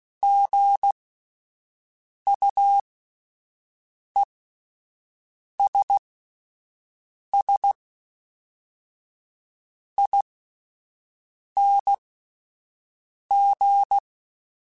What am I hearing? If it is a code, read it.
Morse code, heard as a keyed tone, about 16 wpm, spaced as for 5 wpm, 800 Hz. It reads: GUESSING